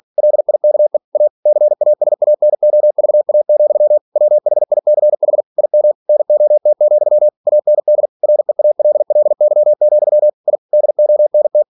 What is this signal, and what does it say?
Morse code, heard as a keyed tone, about 46 wpm, 600 Hz. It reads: LIKE A CASANOVA, WHICH IM NOT, AND REALLY, I DONT
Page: Never 47wpm